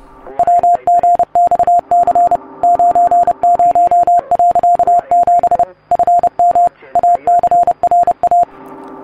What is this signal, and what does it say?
Morse code, heard as a keyed tone, about 30 wpm, 700 Hz. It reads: WGXC 90.7 FM ACRA